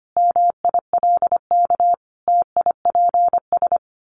Morse code, heard as a keyed tone, about 25 wpm, 700 Hz. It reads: MILK TIPS
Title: Additional Easy Phrases 25wpm